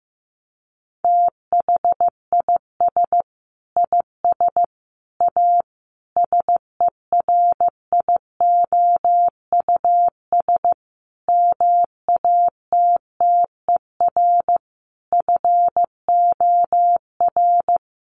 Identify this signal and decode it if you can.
Morse code, heard as a keyed tone, about 15 wpm, 700 Hz. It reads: THIS IS A SERIOUS MATTER FOR
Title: Sentences from Top 1100 Words 15wpm